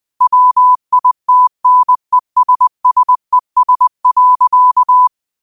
Morse code, heard as a keyed tone, about 20 wpm, 1000 Hz. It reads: WITNESSES.